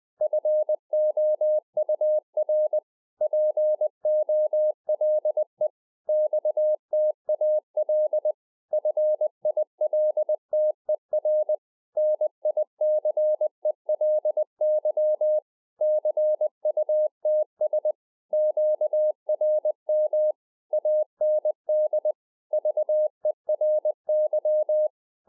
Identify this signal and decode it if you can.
Morse code, heard as a keyed tone, about 20 wpm, 610 Hz. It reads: FOUR POLE XTAL FILTER NICELY CUTS QRM AND VERY
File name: 2011-04-mp3 / A SMALL TRANCEIVER 3 20WPM